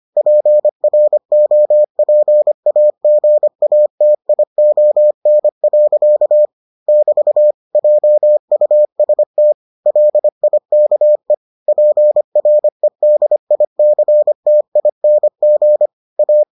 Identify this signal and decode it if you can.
Morse code, heard as a keyed tone, about 25 wpm, 600 Hz. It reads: PROPAGATION. <BT> JUST LIKE PREDICTING A